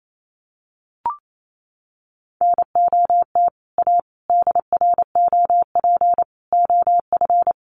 Morse code, heard as a keyed tone, about 28 wpm, 700 Hz. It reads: NOT A DROP OF
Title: Sentences from Top 1400 Words 28wpm